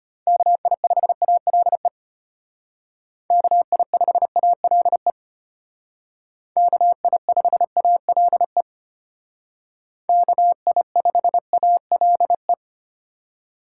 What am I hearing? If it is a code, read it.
Morse code, heard as a keyed tone, about 32 wpm, 700 Hz. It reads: KI5ALE KI5ALE KI5ALE KI5ALE